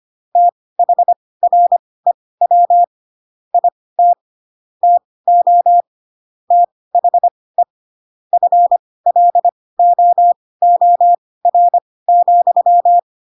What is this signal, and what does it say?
Morse code, heard as a keyed tone, about 25 wpm, spaced as for 18 wpm, 700 Hz. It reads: THREW IT TO THE FLOOR,
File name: Pinocchio / Pinocchio ch1 12 25 ik1hge